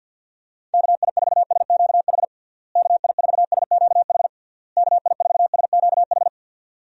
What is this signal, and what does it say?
Morse code, heard as a keyed tone, about 50 wpm, 700 Hz. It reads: KI4SQH KI4SQH KI4SQH
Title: USA Call Signs - 6 Characters - Repeated 3x - 50wpm